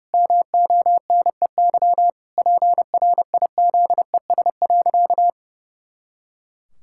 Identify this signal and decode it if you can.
Morse code, heard as a keyed tone, about 30 wpm, 700 Hz. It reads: MONEY PRIZES.
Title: Pangram Challenge v2 - 30wpm